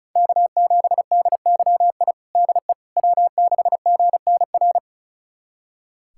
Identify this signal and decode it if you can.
Morse code, heard as a keyed tone, about 34 wpm, 700 Hz. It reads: K7DYI DE W6GNR
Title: USA Callsign DE Callsign - 5 Characters - 35wpm